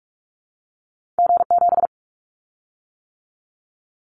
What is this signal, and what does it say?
Morse code, heard as a keyed tone, about 44 wpm, 700 Hz. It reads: G7
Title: Callsign Prefixes for Top 25 DXCC - 45wpm